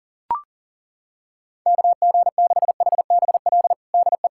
Morse code, heard as a keyed tone, about 40 wpm, 700 Hz. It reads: KG6HBL DE
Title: K1USN SST QSO Practice 40wpm